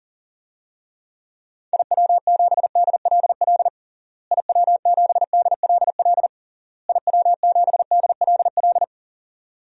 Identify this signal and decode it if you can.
Morse code, heard as a keyed tone, about 40 wpm, 700 Hz. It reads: IW7DLL IW7DLL IW7DLL
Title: DXCC-100 Call Signs - 6 Characters - Repeated 3x - 40wpm